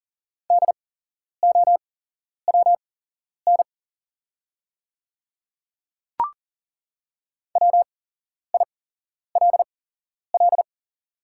Morse code, heard as a keyed tone, about 40 wpm, spaced as for 11 wpm, 690 Hz. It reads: DOWN WILL
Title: 8x Character Spacing - Top 100 Words - 40wpm